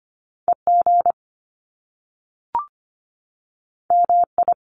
Morse code, heard as a keyed tone, about 25 wpm, 700 Hz. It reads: EG MI
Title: Top Two Letter Combinations in Words 25wpm